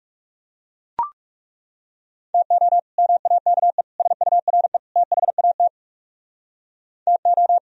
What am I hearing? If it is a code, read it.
Morse code, heard as a keyed tone, about 40 wpm, 700 Hz. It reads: TO MAKE SURE THAT TO